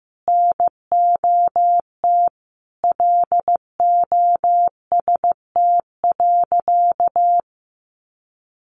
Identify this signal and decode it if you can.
Morse code, heard as a keyed tone, about 15 wpm, 700 Hz. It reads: NOT LOST.
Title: Sentences from Top 1400 Words 15wpm